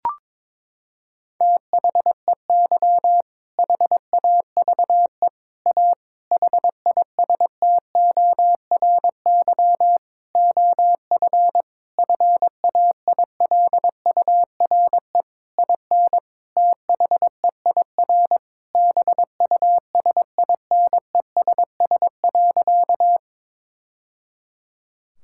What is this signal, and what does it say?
Morse code, heard as a keyed tone, about 22 wpm, 700 Hz. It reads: THEY HAVE A HISTORY OF FAILURE IN THEIR BUSINESS.
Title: Sentences from Top 1000 Words - Review Part 6 - 22wpm